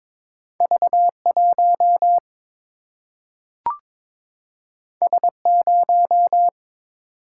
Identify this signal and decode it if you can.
Morse code, heard as a keyed tone, about 22 wpm, 700 Hz. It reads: V 1 S 0
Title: Single-Letter Single-Number 22wpm